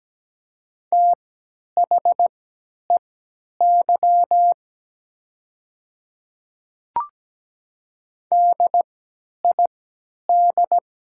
Morse code, heard as a keyed tone, about 17 wpm, spaced as for 10 wpm, 700 Hz. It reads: THEY DID
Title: 3x Character Spacing - Top 200 Words - 17wpm